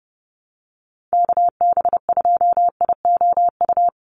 Morse code, heard as a keyed tone, about 30 wpm, 700 Hz. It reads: KB2IOU
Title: USA Call Signs 30wpm